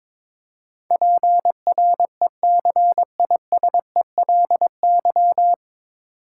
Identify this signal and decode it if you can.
Morse code, heard as a keyed tone, about 22 wpm, 700 Hz. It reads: PRECISELY
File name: 9 Letter Words - First 3 Letters Given - 22wpm